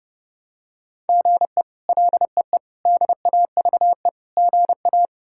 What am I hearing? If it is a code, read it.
Morse code, heard as a keyed tone, about 30 wpm, 700 Hz. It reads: GE LEE DAVE GA
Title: K1USN SST QSO Practice - 30wpm